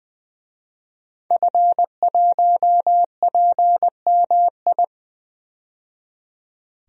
Morse code, heard as a keyed tone, about 20 wpm, 700 Hz. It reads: F1PMI
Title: DXCC-100 Call Signs - 5 Characters - 20wpm